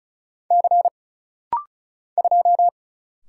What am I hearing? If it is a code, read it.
Morse code, heard as a keyed tone, about 34 wpm, 700 Hz. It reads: C 2
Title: Single Letter-Number - Rapid-Fire 35wpm